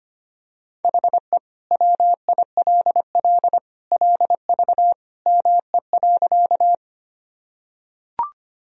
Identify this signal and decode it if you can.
Morse code, heard as a keyed tone, about 25 wpm, 700 Hz. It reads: HE WILL LV ME.
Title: Sentences from Top 400 Words and Abbreviations 25wpm